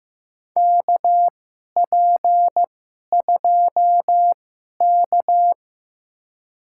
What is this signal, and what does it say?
Morse code, heard as a keyed tone, about 15 wpm, spaced as for 11 wpm, 700 Hz. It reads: KP2K
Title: USA Call Signs - 4 Characters - 2x Character Spacing - 15wpm